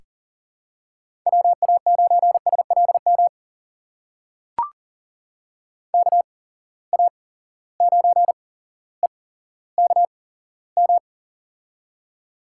Morse code, heard as a keyed tone, about 39 wpm, spaced as for 17 wpm, 700 Hz. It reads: WA9SLM KA9EKM
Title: USA Call Signs - 6 Characters - 8x Character Spacing - 40wpm